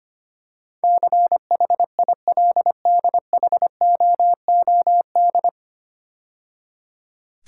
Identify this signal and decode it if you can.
Morse code, heard as a keyed tone, about 25 wpm, 700 Hz. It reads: CHILDHOOD
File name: practice / 25wpm / 9 Letter Words - First 3 Letters Given - 25wpm